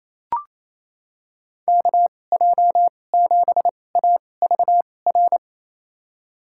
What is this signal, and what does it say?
Morse code, heard as a keyed tone, about 28 wpm, spaced as for 20 wpm, 700 Hz. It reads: KJ7AVR
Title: USA Call Signs - 2x Character Spacing - 28wpm